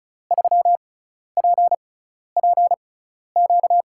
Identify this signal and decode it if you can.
Morse code, heard as a keyed tone, about 34 wpm, spaced as for 12 wpm, 700 Hz. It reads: 3PPQ
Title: USA Call Signs - 6 Characters - 6x Character Spacing - 35wpm